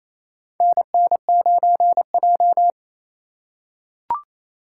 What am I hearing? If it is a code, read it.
Morse code, heard as a keyed tone, about 28 wpm, 700 Hz. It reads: NN9J
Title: USA Call Signs - 4 Characters - 4x Character Spacing - 28wpm